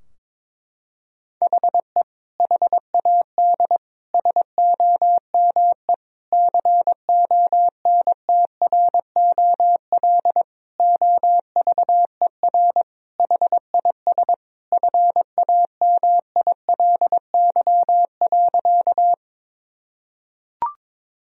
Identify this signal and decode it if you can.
Morse code, heard as a keyed tone, about 22 wpm, 700 Hz. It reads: HE HAD SOME CONTROL OVER HIS FAMILY.
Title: Sentences from Top 1000 Words - Review Part 3 - 22wpm